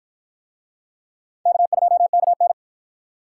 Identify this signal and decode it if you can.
Morse code, heard as a keyed tone, about 53 wpm, 700 Hz. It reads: K2KN